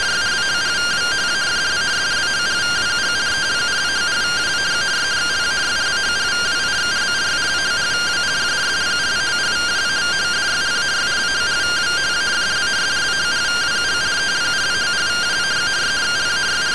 Two-tone teleprinter signal, about 170 Hz shift.
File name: findings / CT AXIAL -377.gen-08 rtty rx